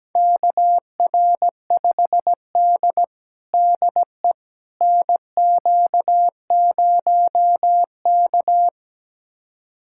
Morse code, heard as a keyed tone, about 17 wpm, 700 Hz. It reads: KR5D DE NQ0K
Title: USA Callsign DE Callsign - 4 Characters - 17wpm